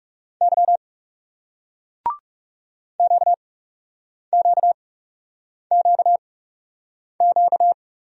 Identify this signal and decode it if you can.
Morse code, heard as a keyed tone, about 40 wpm, spaced as for 8 wpm, 700 Hz. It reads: Y QQQQ